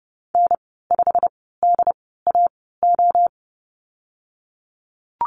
Morse code, heard as a keyed tone, about 30 wpm, 700 Hz. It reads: N 5 D A O
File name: Lesson 19g - Letter w - Call Signs 30wpm